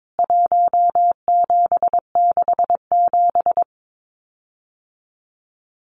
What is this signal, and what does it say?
Morse code, heard as a keyed tone, about 22 wpm, 700 Hz. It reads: 1767
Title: CWT Exchange 22wpm